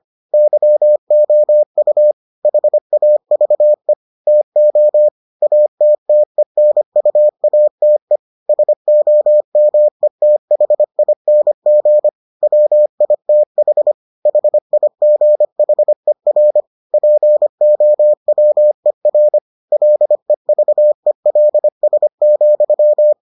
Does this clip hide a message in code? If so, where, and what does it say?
Yes: Morse code from 0.3 to 23.1 s, heard as a keyed tone, about 25 wpm, 600 Hz. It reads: YOU HAVE TO ATTENUATE SOMETHING WITH HIGHER POWER LEVELS,